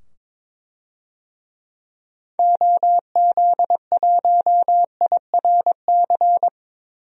Morse code, heard as a keyed tone, about 21 wpm, 700 Hz. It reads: OZ1IRC
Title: DXCC-100 Call Signs - 6 Characters - 22wpm